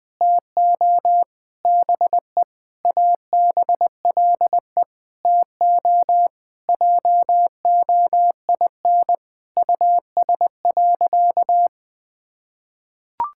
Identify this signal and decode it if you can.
Morse code, heard as a keyed tone, about 19 wpm, 700 Hz. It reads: TO BE ABLE TO JOIN US.
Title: Sentences with 5-Word Phrases - 20wpm